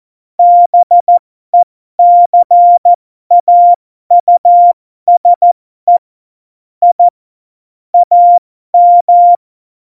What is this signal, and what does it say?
Morse code, heard as a keyed tone, about 14 wpm, spaced as for 12 wpm, 700 Hz. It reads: BECAUSE I AM
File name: Morsekurs / Witze